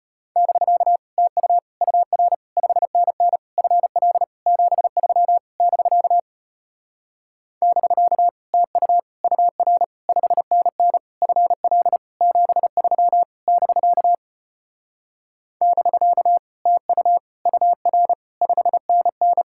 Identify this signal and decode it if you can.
Morse code, heard as a keyed tone, about 33 wpm, 700 Hz. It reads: <BK> TU UR 5NN FL 73 <BK> <BK> TU UR 5NN FL 73 <BK> <BK> TU UR 5NN